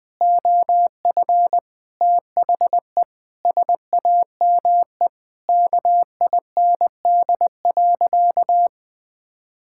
Morse code, heard as a keyed tone, about 20 wpm, 700 Hz. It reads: OF THE SAME KIND.
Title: Sentences with 3-Word Phrases - 20wpm